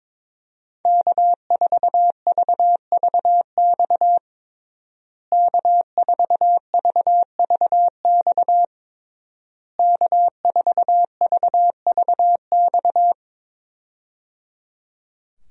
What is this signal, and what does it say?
Morse code, heard as a keyed tone, about 22 wpm, 700 Hz. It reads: K4VVX K4VVX K4VVX
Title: USA Call Signs - 5 Characters - Repeated 3x - 22wpm